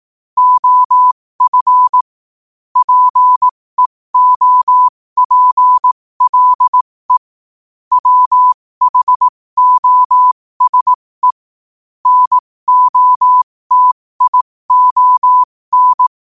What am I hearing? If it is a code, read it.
Morse code, heard as a keyed tone, about 18 wpm, spaced as for 15 wpm, 1000 Hz. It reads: OF PEOPLE WHOSE NOTION
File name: qotdfiles / 15wpm